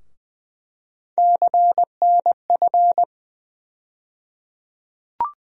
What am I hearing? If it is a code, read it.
Morse code, heard as a keyed tone, about 21 wpm, 700 Hz. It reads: CNF
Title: USA Call Sign Postfixes - 3 Letters 20wpm